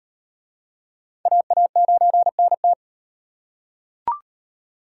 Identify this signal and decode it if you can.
Morse code, heard as a keyed tone, about 37 wpm, 700 Hz. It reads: AA9NT